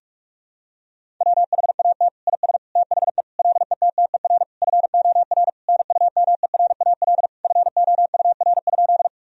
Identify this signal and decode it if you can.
Morse code, heard as a keyed tone, about 46 wpm, 700 Hz. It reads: WHAT IS THE LETTER FOR NUMERAL FOUR?
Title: Sentences from Top 1000 Words - Review Part 2 - 45wpm